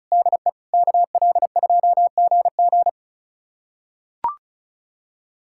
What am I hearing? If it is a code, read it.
Morse code, heard as a keyed tone, about 34 wpm, 700 Hz. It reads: DE KL2GG
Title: USA Callsign DE Callsign 35wpm